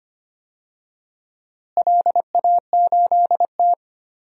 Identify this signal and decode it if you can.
Morse code, heard as a keyed tone, about 25 wpm, 700 Hz. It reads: LA8T
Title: DXCC-100 Call Signs - 4 Characters - Repeated 3x - 25wpm